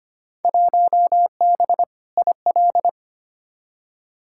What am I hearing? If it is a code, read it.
Morse code, heard as a keyed tone, about 25 wpm, 700 Hz. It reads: 1B IL
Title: Field Day Exchanges - 25wpm